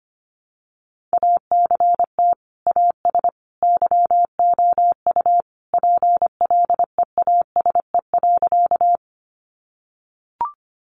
Morse code, heard as a keyed tone, about 25 wpm, 700 Hz. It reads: ACT AS YOU PLEASE.
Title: Sentences from Top 1500 Words 25wpm